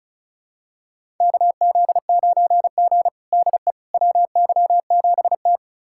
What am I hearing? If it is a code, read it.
Morse code, heard as a keyed tone, about 36 wpm, 700 Hz. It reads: KZ9G DE WY7T
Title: USA Callsign DE Callsign - 4 Characters - 35wpm